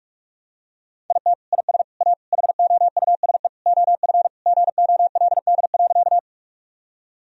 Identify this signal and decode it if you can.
Morse code, heard as a keyed tone, about 46 wpm, 700 Hz. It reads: IT IS A HOUSE OF GOLD.
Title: Sentences from Top 1000 Words - Review Part 3 - 45wpm